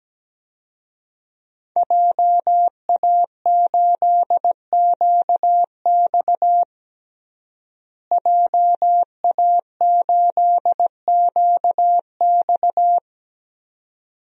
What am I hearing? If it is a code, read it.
Morse code, heard as a keyed tone, about 17 wpm, 700 Hz. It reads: JA8QX JA8QX
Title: DXCC-100 Call Signs - 5 Characters - Repeated 2x - 17wpm